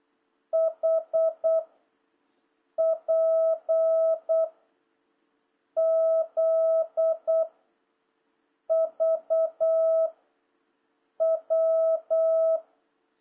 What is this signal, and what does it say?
Morse code, heard as a keyed tone, about 8 wpm, 650 Hz. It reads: H P Z V W
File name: jh8cbh / 2026cw